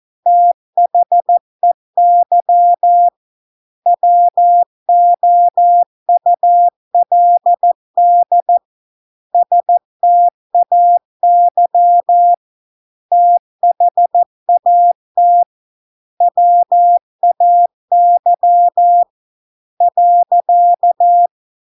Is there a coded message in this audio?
Yes: Morse code from 0.3 to 21.3 s, heard as a keyed tone, about 14 wpm, 700 Hz. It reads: THEY WOULD STAY THAT WAY .